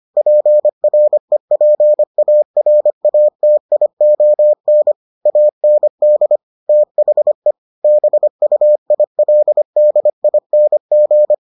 Morse code, heard as a keyed tone, about 25 wpm, 600 Hz. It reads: PREPARATION AND THE BUILDING